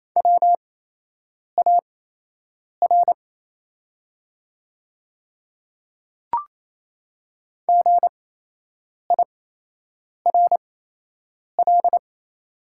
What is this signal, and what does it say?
Morse code, heard as a keyed tone, about 28 wpm, spaced as for 8 wpm, 690 Hz. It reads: WAR GIRL
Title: 8x Character Spacing - Top 500 Words - 28wpm